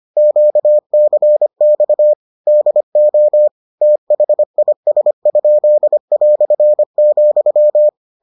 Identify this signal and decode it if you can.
Morse code, heard as a keyed tone, about 25 wpm, 600 Hz. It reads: QCX DO THIS?",